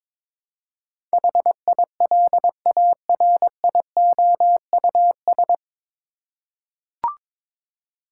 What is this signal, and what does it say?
Morse code, heard as a keyed tone, about 22 wpm, 700 Hz. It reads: HILARIOUS
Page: Morse Code Ninja practice 22wpm